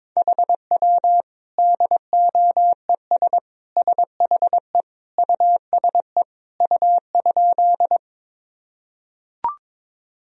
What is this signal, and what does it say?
Morse code, heard as a keyed tone, about 22 wpm, 700 Hz. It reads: HW DOES SHE USE U?
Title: Sentences from Top 200 Words and Abbreviations 22wpm